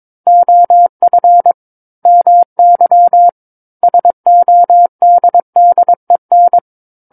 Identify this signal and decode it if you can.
Morse code, heard as a keyed tone, about 22 wpm, 700 Hz. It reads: OF MY SODDEN
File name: Book One / 13-Chapter 13 - 22 WPM